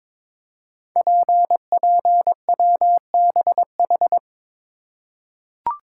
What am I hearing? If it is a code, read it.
Morse code, heard as a keyed tone, about 23 wpm, 700 Hz. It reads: PPWBH
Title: Code Groups - 5 Letters - 22wpm